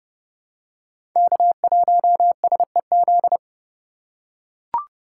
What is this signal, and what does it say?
Morse code, heard as a keyed tone, about 32 wpm, 700 Hz. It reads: K1SEZ
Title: USA Call Signs - 5 Characters - 3x Character Spacing - 30wpm